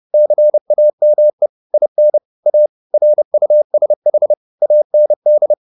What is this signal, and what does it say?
Morse code, heard as a keyed tone, about 30 wpm, 600 Hz. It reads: CAME IN A RUSH AND